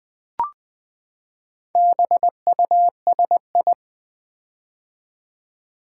Morse code, heard as a keyed tone, about 20 wpm, 700 Hz. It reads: BUSI
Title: Build A Word - 4 Letters At A Time - 11-Letter Words - 20wpm